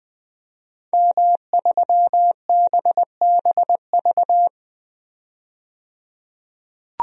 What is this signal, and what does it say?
Morse code, heard as a keyed tone, about 20 wpm, 700 Hz. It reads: M3BBV
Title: DXCC-100 Call Signs - 5 Characters - 20wpm